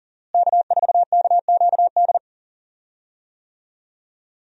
Morse code, heard as a keyed tone, about 40 wpm, 700 Hz. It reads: K4KQD
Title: USA Call Signs - 5 Characters - Repeated 3x - 40wpm